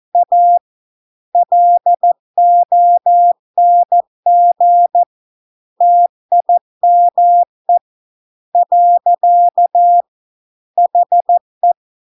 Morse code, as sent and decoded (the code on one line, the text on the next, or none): .- / .-.. --- -. --. / - .. -- . / .-.-.- / .... .
A LONG TIME . HE